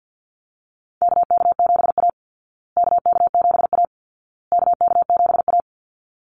Morse code, heard as a keyed tone, about 50 wpm, 700 Hz. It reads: KK7A KK7A KK7A
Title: Call Signs Repeated 2x - 50wpm